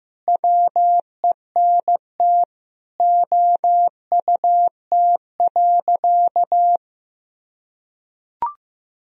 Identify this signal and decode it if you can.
Morse code, heard as a keyed tone, about 15 wpm, 700 Hz. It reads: WENT OUT.
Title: Sentences from Top 1000 Words - Review Part 1 - 15wpm